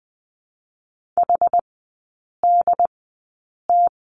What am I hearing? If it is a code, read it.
Morse code, heard as a keyed tone, about 20 wpm, spaced as for 8 wpm, 700 Hz. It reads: HDT
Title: 067 Lesson 12e - Letter d - Three Characters 20wpm